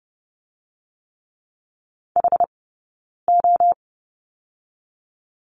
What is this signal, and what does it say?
Morse code, heard as a keyed tone, about 30 wpm, 700 Hz. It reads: H O
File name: Lesson 16d - Letter c - Two Characters 30wpm